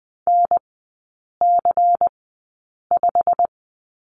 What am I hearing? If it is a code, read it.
Morse code, heard as a keyed tone, about 20 wpm, spaced as for 8 wpm, 690 Hz. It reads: NC5
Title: 159 Lesson 26e - Letter g - Three Characters 20wpm